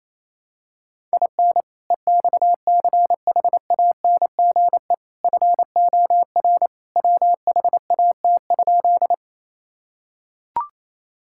Morse code, heard as a keyed tone, about 28 wpm, 700 Hz. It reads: IN EXCHANGE FOR WHAT?
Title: Sentences from Top 1000 Words - Review Part 6 - 28wpm